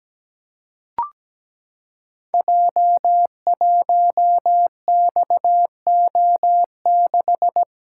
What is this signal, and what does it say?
Morse code, heard as a keyed tone, about 17 wpm, 700 Hz. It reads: J1XO6
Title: Code Groups - 5 Letter:Numbers - 17wpm